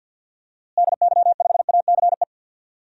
Morse code, heard as a keyed tone, about 52 wpm, 700 Hz. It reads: DY5ACE
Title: DXCC-100 Call Signs - 6 Characters - Repeated 2x - 50wpm